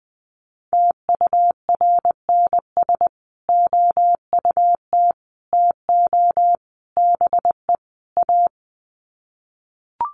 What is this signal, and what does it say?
Morse code, heard as a keyed tone, about 20 wpm, 700 Hz. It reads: TURNS OUT TO BE A
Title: Sets of 5 Words Encore 20wpm